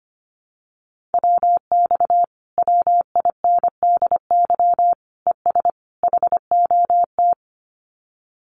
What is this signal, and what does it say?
Morse code, heard as a keyed tone, about 25 wpm, 700 Hz. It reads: WX WINDY ES HOT
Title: Common QSO Phrases Encore 25wpm